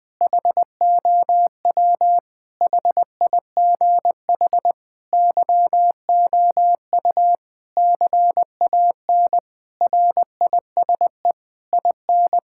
Morse code, heard as a keyed tone, about 21 wpm, 700 Hz. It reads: HOW HIGH YOU CAN RISE IN
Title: Sentences from Top 1000 Words - Review Part 6 - 20wpm